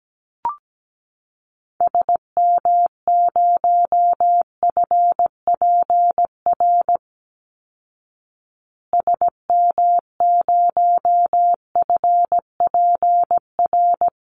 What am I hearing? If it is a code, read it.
Morse code, heard as a keyed tone, about 17 wpm, 700 Hz. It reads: SM0FPR SM0FPR
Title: Call Signs Repeated 2x - 17wpm